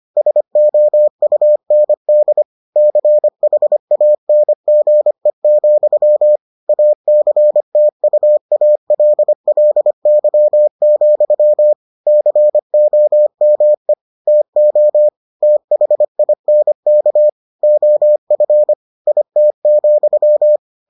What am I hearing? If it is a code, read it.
Morse code, heard as a keyed tone, about 25 wpm, 600 Hz. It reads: SOUND CHANGE, ACTUALLY, COME TO THINK OF IT,